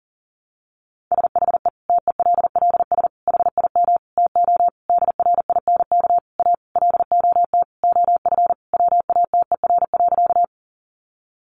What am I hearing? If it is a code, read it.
Morse code, heard as a keyed tone, about 40 wpm, 700 Hz. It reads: SHE TELLS HIM TO DRINK A LOT OF WATER.
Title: Additional Easy Phrases 40wpm